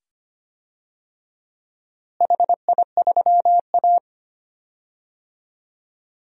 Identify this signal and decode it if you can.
Morse code, heard as a keyed tone, about 25 wpm, 700 Hz. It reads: HI3A
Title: DXCC-100 Call Signs - 4 Characters - Repeated 2x - 25wpm